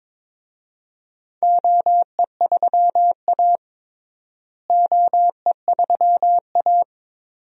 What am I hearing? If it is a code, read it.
Morse code, heard as a keyed tone, about 22 wpm, 700 Hz. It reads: OE3A OE3A
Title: DXCC-100 Call Signs - 4 Characters - Repeated 2x - 22wpm